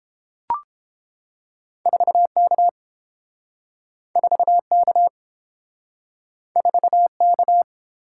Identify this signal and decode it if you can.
Morse code, heard as a keyed tone, about 30 wpm, 700 Hz. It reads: 4K 4K 4K